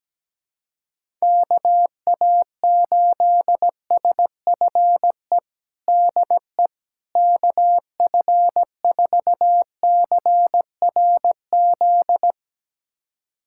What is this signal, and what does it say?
Morse code, heard as a keyed tone, about 17 wpm, 700 Hz. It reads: KA8SFE DE KF4CRZ
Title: USA Callsign DE Callsign - 6 Characters - 17wpm